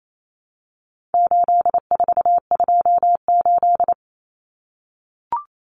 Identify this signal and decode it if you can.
Morse code, heard as a keyed tone, about 28 wpm, 700 Hz. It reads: 8428
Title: Numbers - 4-Digit 28wpm